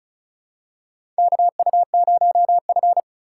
Morse code, heard as a keyed tone, about 36 wpm, 700 Hz. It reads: KU0F
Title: USA Call Signs - 4 Characters - 4x Character Spacing - 35wpm